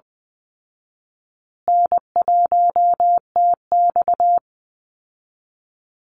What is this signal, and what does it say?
Morse code, heard as a keyed tone, about 20 wpm, 700 Hz. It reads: N1TX
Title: Call Signs Repeated 3x - 20wpm